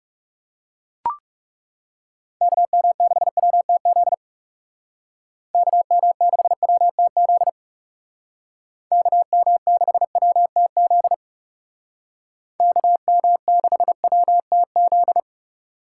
Morse code, heard as a keyed tone, about 37 wpm, 700 Hz. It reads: KM6WTZ KM6WTZ KM6WTZ KM6WTZ
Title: Speed Racing - USA Call Signs 30wpm